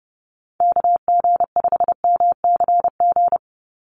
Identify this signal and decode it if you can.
Morse code, heard as a keyed tone, about 30 wpm, 700 Hz. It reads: KG5MCG
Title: USA Call Signs 30wpm